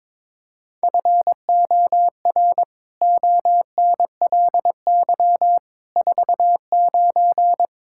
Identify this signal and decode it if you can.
Morse code, heard as a keyed tone, about 22 wpm, 700 Hz. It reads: FOR ONLY 49
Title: Pangram Challenge v2 - 22wpm